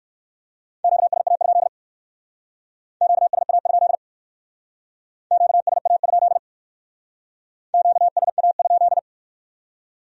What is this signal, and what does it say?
Morse code, heard as a keyed tone, about 56 wpm, 700 Hz. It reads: QSA? QSA? QSA? QSA?
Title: Speed-Racing Q-Codes - 45wpm